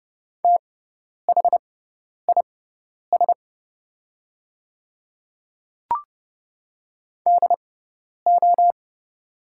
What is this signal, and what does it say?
Morse code, heard as a keyed tone, about 29 wpm, spaced as for 10 wpm, 690 Hz. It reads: THIS DO
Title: 6x Character Spacing - Top 100 Words - 30wpm